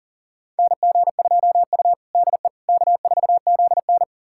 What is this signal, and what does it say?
Morse code, heard as a keyed tone, about 37 wpm, 700 Hz. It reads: NG2U DE K4ZN
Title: USA Callsign DE Callsign - 4 Characters - 40wpm